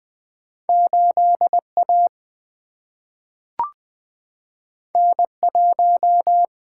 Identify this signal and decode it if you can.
Morse code, heard as a keyed tone, about 20 wpm, 700 Hz. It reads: 8A N1
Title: Single-Letter Single-Number 20wpm